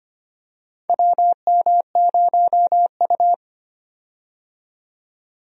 Morse code, heard as a keyed tone, about 25 wpm, 700 Hz. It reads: WM0U
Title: USA Call Signs - 4 Characters - 25wpm